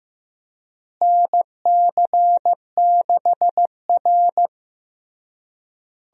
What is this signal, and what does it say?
Morse code, heard as a keyed tone, about 15 wpm, 700 Hz. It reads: NC6R
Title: USA Call Signs - 4 Characters - Repeated 3x - 15wpm